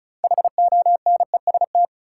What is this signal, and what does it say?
Morse code, heard as a keyed tone, about 36 wpm, 700 Hz. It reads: HONEST
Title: Sets of 5 Words Encore - New Word Drill - 35wpm